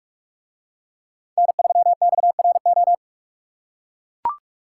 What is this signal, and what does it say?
Morse code, heard as a keyed tone, about 43 wpm, 700 Hz. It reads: N3XRO